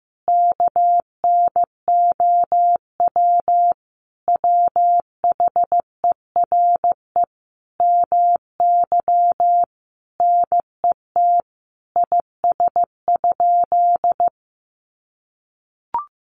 Morse code, heard as a keyed tone, about 15 wpm, 700 Hz. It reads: KNOW WHERE MY NET IS?
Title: Sentences from Top 1400 Words 15wpm